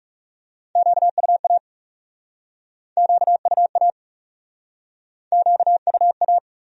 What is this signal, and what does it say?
Morse code, heard as a keyed tone, about 40 wpm, 700 Hz. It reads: QUA QUA QUA